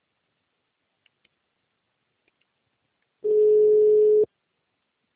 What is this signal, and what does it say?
Ringback tone (Europe, 425 Hz).